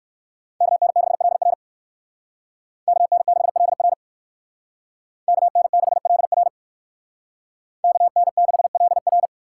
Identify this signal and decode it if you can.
Morse code, heard as a keyed tone, about 58 wpm, 700 Hz. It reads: KN6LR KN6LR KN6LR KN6LR